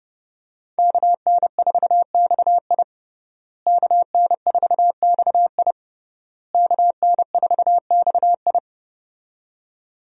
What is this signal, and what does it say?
Morse code, heard as a keyed tone, about 32 wpm, 700 Hz. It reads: KN4XI KN4XI KN4XI
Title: USA Call Signs - 5 Characters - Repeated 3x - 30wpm